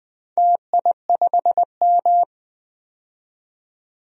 Morse code, heard as a keyed tone, about 20 wpm, 700 Hz. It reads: TI5M
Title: DXCC-100 Call Signs - 4 Characters - 20wpm